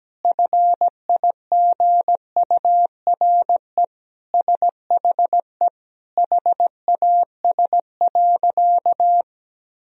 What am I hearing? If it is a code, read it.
Morse code, heard as a keyed tone, about 17 wpm, 700 Hz. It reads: FIGURE SHE HAS.
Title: Sentences from Top 1000 Words - Review Part 2 - 17wpm